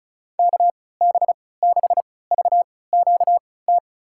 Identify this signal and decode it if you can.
Morse code, heard as a keyed tone, about 34 wpm, spaced as for 20 wpm, 700 Hz. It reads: KB6VQT
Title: USA Call Signs - 6 Characters - 3x Character Spacing - 35wpm